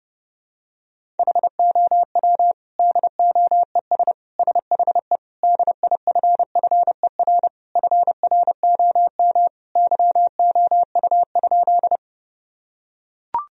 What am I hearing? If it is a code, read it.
Morse code, heard as a keyed tone, about 30 wpm, 700 Hz. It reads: HOW DOES SHE DIFFER FROM YOU?
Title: Sentences from Top 1000 Words - Review Part 1 - 30wpm